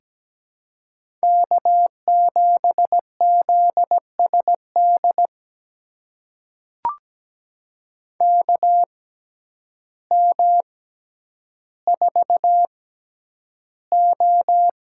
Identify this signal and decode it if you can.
Morse code, heard as a keyed tone, about 17 wpm, spaced as for 9 wpm, 700 Hz. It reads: K7ZSD KM4O
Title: USA Call Signs - 6x Character Spacing - 17wpm